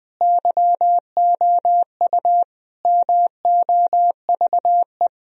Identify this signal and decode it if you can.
Morse code, heard as a keyed tone, about 21 wpm, 700 Hz. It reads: YOU MOVE
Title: Sentences from Top 1000 Words - Review Part 1 - 20wpm